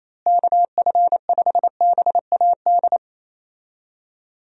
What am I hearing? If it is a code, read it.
Morse code, heard as a keyed tone, about 28 wpm, 700 Hz. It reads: KF5BAD
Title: USA Call Signs - 6 Characters - Repeated 2x - 28wpm